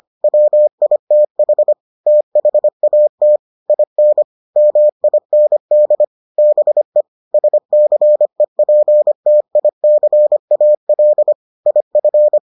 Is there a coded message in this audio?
Yes: Morse code from 0.2 to 12.4 s, heard as a keyed tone, about 25 wpm, 600 Hz. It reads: WITH THAT IN MIND BE SCEPTICAL IF